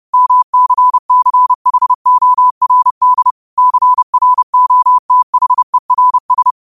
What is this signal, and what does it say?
Morse code, heard as a keyed tone, about 30 wpm, 1000 Hz. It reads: MCCHORD CROTHERS
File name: qotdfiles / 30wpm